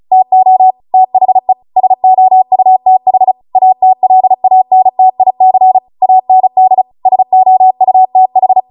Morse code, heard as a keyed tone, about 36 wpm, 750 Hz. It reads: TO THE SOUTH ATLANTIC AND SOUTH